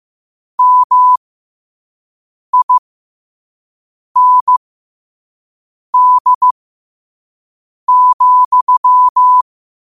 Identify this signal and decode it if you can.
Morse code, heard as a keyed tone, about 15 wpm, spaced as for 5 wpm, 1000 Hz. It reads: MIND,
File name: qotdfiles / 5wpm